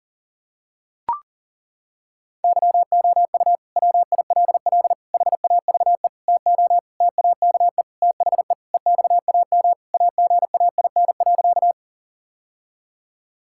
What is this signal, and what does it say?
Morse code, heard as a keyed tone, about 40 wpm, 700 Hz. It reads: YOU WILL HAVE TO TAKE THE EXAM AGAIN.
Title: Sentences from Top 1000 Words - Review Part 5 - 40wpm